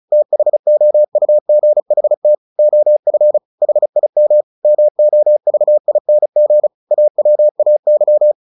Morse code, heard as a keyed tone, about 37 wpm, 600 Hz. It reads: THOUGHT OF HIM MOVING AWAY
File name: the old man and the sea / the old man and the sea 1 35wpm